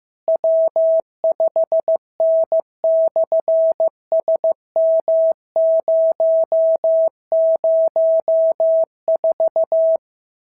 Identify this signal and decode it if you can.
Morse code, heard as a keyed tone, about 15 wpm, 650 Hz. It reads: W5N/SM004
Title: SOTA QSO Practice 15wpm